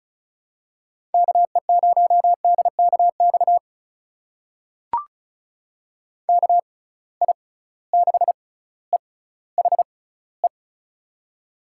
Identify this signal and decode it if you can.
Morse code, heard as a keyed tone, about 36 wpm, spaced as for 18 wpm, 700 Hz. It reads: KE0DKX KI6EHE
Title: USA Call Signs - 6 Characters - 6x Character Spacing - 35wpm